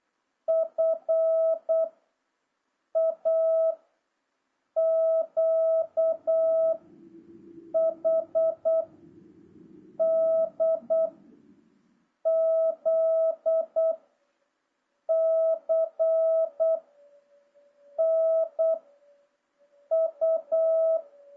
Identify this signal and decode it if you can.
Morse code, heard as a keyed tone, about 8 wpm, 650 Hz. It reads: F A Q H D Z C N U
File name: jh8cbh / 2026cw